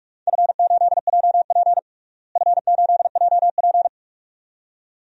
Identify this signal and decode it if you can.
Morse code, heard as a keyed tone, about 46 wpm, 700 Hz. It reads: F8JP F8JP
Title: DXCC-100 Call Signs - 4 Characters - Repeated 2x - 45wpm